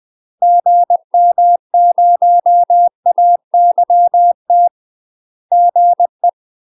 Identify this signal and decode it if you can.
Morse code, heard as a keyed tone, about 20 wpm, 700 Hz. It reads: GM0AYT GE
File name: UK MASTER QSOs TRAINING / 20WPM / UK MASTER QSOs 0063